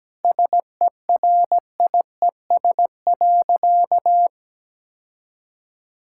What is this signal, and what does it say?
Morse code, heard as a keyed tone, about 17 wpm, 700 Hz. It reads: SERIES.
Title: Sentences from Top 1000 Words - Review Part 5 - 17wpm